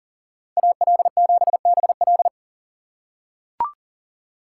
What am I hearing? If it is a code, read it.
Morse code, heard as a keyed tone, about 40 wpm, 700 Hz. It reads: AL7BL
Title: USA Call Signs - 5 Characters - 6x Character Spacing - 40wpm